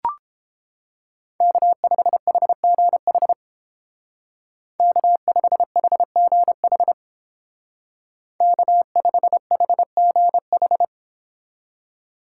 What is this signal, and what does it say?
Morse code, heard as a keyed tone, about 29 wpm, 700 Hz. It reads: K5HGH K5HGH K5HGH